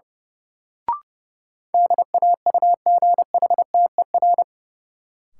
Morse code, heard as a keyed tone, about 32 wpm, 700 Hz. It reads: DAUGHTER
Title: Sentences from Top 1500 Words - New Word Drill - 30wpm